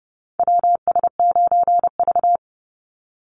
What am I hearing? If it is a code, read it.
Morse code, heard as a keyed tone, about 30 wpm, 700 Hz. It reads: WS9V
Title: Call Signs Repeated 2x - 30wpm